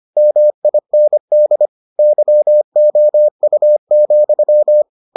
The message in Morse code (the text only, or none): MIND YOU,